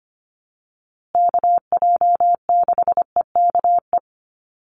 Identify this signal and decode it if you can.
Morse code, heard as a keyed tone, about 25 wpm, 700 Hz. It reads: KJ6EKE
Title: USA Call Signs Repeated 2x 25wpm